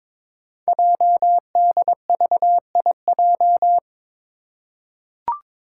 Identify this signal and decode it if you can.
Morse code, heard as a keyed tone, about 22 wpm, 700 Hz. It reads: JDVIJ
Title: Code Groups - 5 Letter:Numbers - 22wpm